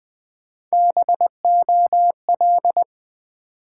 Morse code, heard as a keyed tone, about 20 wpm, 700 Hz. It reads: BOL
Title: Build A Word - 1 Letter At A Time - 3 to 5-Letter Words - 20wpm